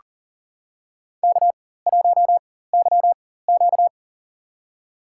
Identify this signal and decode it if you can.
Morse code, heard as a keyed tone, about 40 wpm, spaced as for 19 wpm, 700 Hz. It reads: K1YQ
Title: USA Call Signs - 4 Characters - 4x Character Spacing - 40wpm